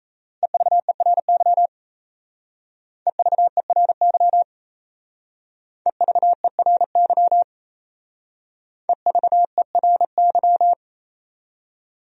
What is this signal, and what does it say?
Morse code, heard as a keyed tone, about 34 wpm, 700 Hz. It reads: EVERY EVERY EVERY EVERY
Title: Speed-Racing Top 300 Words 28wpm